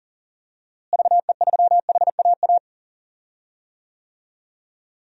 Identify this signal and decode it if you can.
Morse code, heard as a keyed tone, about 38 wpm, 700 Hz. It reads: VE3HAA
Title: DXCC-100 Call Signs - 6 Characters - Repeated 2x - 40wpm